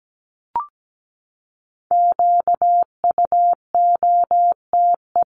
Morse code, heard as a keyed tone, about 17 wpm, 700 Hz. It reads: QUOTE
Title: Top 1751-2000 Words 17wpm